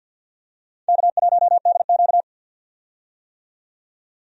Morse code, heard as a keyed tone, about 51 wpm, 700 Hz. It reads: K1DQ
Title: USA Call Signs - 4 Characters - 50wpm